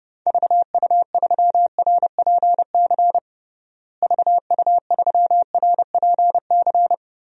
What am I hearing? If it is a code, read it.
Morse code, heard as a keyed tone, about 30 wpm, 700 Hz. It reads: VU3RPC VU3RPC
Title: DXCC-100 Call Signs - 6 Characters - Repeated 2x - 30wpm